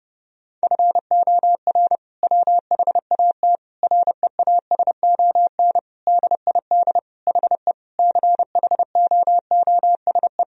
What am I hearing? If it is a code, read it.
Morse code, heard as a keyed tone, about 30 wpm, 700 Hz. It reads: FOR WHAT REASON DID HE CHOOSE
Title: Sentences from Top 1000 Words - Review Part 4 - 30wpm